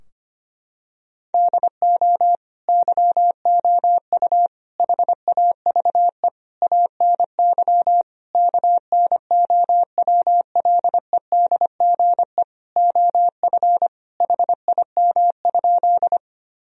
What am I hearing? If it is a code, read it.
Morse code, heard as a keyed tone, about 25 wpm, 700 Hz. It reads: DO YOU HAVE ANY KNOWLEDGE OF HIM?
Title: Sentences from Top 1000 Words - Review Part 3 - 25wpm